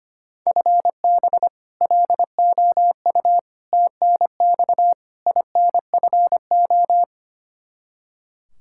Morse code, heard as a keyed tone, about 25 wpm, 700 Hz. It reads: FB LOU TNX INFO
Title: K1USN SST QSO Practice 25wpm